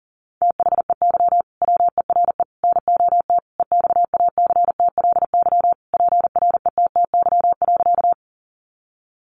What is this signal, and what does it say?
Morse code, heard as a keyed tone, about 40 wpm, 700 Hz. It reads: THEY WERE NOT EXACTLY PRETTY.
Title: Sentences from Top 1000 Words 40wpm